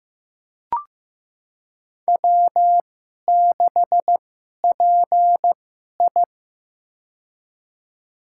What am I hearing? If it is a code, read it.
Morse code, heard as a keyed tone, about 15 wpm, spaced as for 11 wpm, 700 Hz. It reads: W6PI
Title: USA Call Signs - 2x Character Spacing - 15wpm